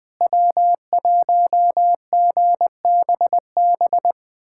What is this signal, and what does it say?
Morse code, heard as a keyed tone, about 20 wpm, 700 Hz. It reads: W1GBB
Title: USA Callsign DE Callsign 20wpm